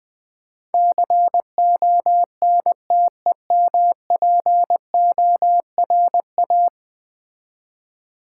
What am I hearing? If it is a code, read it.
Morse code, heard as a keyed tone, about 20 wpm, 700 Hz. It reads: CONTEMPORA
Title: Build A Word - 5 Letters At A Time - 12-Letter Words - 20wpm